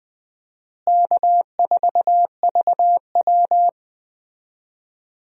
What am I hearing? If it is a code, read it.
Morse code, heard as a keyed tone, about 20 wpm, 700 Hz. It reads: K4VW
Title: USA Call Signs - 4 Characters - Repeated 2x - 20wpm